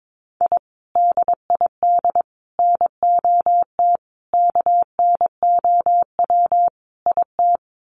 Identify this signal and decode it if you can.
Morse code, heard as a keyed tone, about 22 wpm, 700 Hz. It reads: I DID NOT KNOW IT
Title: Sets of 5 Words Encore2 22wpm